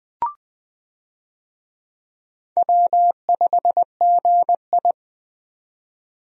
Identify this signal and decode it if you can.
Morse code, heard as a keyed tone, about 20 wpm, 700 Hz. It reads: W5GI
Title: USA Call Signs - 4 Characters - 20wpm